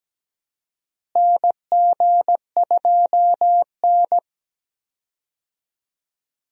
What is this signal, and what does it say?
Morse code, heard as a keyed tone, about 17 wpm, 700 Hz. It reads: NG2N
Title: USA Call Signs - 4 Characters - Repeated 2x - 17wpm